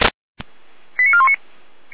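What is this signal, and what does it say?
Five-tone selective call (ZVEI), heard as quick runs of five tones, tones of about 70 ms. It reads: 89419
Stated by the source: Received by SDR, ZVEI five-tone selective call.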